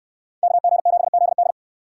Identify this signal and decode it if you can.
Morse code, heard as a keyed tone, about 69 wpm, 700 Hz. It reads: KG7ZD